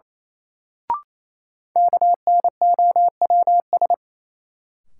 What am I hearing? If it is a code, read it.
Morse code, heard as a keyed tone, about 28 wpm, 700 Hz. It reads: KNOWS
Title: Sentences from Top 1000 Words - Review Part 1 - New Word Drill - 28wpm